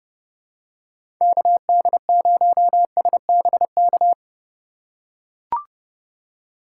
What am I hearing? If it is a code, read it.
Morse code, heard as a keyed tone, about 32 wpm, 700 Hz. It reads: KD0SBK
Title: USA Call Signs - 2x Character Spacing - 30wpm